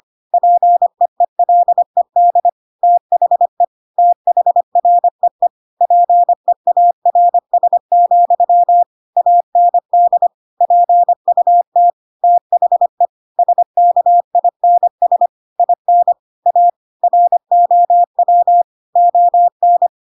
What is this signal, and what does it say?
Morse code, heard as a keyed tone, about 25 wpm, 700 Hz. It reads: PEELED THE THREE PEARS, AND PUT THE SKINS IN A ROW ON
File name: Pinocchio / Pinocchio ch7 25 25 ik1hge